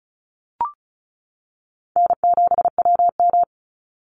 Morse code, heard as a keyed tone, about 36 wpm, 700 Hz. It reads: N7WM
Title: NA SOTA Chaser Callsigns 35wpm